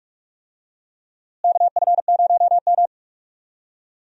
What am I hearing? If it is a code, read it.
Morse code, heard as a keyed tone, about 46 wpm, 700 Hz. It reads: KF0M
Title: USA Call Signs - 4 Characters - Repeated 2x - 45wpm